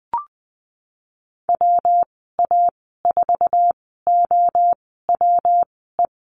Morse code, heard as a keyed tone, about 20 wpm, spaced as for 15 wpm, 700 Hz. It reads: WA4OWE
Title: 128 Lesson 21g - Number 6 - Call Signs 20wpm